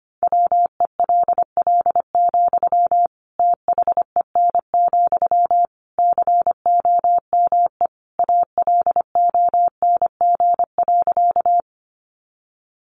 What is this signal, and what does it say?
Morse code, heard as a keyed tone, about 25 wpm, 700 Hz. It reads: WELL, THEN, COME ALONG.
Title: Sentences from Top 700 Words 25wpm